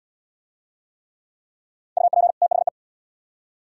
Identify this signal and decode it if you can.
Morse code, heard as a keyed tone, about 67 wpm, 700 Hz. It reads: UP THE